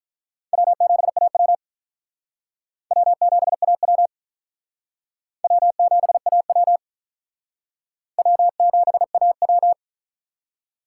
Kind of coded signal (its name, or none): Morse code